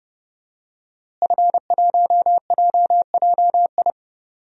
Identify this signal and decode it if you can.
Morse code, heard as a keyed tone, about 31 wpm, 700 Hz. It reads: F1JJI
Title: DXCC-100 Call Signs - 5 Characters - Repeated 3x - 30wpm